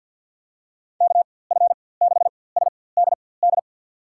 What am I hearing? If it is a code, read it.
Morse code, heard as a keyed tone, about 52 wpm, spaced as for 23 wpm, 700 Hz. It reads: KF6SDD
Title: USA Call Signs - 6 Characters - 4x Character Spacing - 50wpm